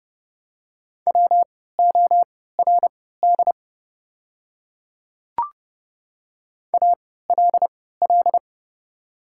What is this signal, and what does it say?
Morse code, heard as a keyed tone, about 30 wpm, spaced as for 17 wpm, 700 Hz. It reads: WORD ALL
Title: 3x Character Spacing - Top 100 Words - 30wpm